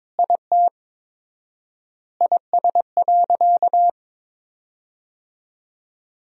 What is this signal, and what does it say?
Morse code, heard as a keyed tone, about 22 wpm, 700 Hz. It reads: IT IS.
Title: 4x Word-Spacing - Sentences from Top 200 Words - 22wpm